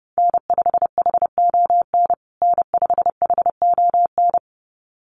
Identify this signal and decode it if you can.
Morse code, heard as a keyed tone, about 30 wpm, 700 Hz. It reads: N5HON N5HON
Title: USA Call Signs Repeated 2x 30wpm